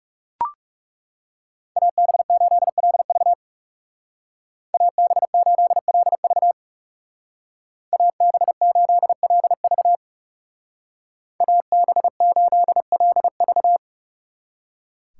Morse code, heard as a keyed tone, about 36 wpm, 700 Hz. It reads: AB8LV AB8LV AB8LV AB8LV
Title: Speed Racing - USA Call Signs 30wpm